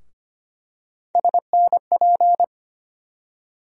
Morse code, heard as a keyed tone, about 25 wpm, 700 Hz. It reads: SNP
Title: USA Call Sign Postfixes - 3 Letters 25wpm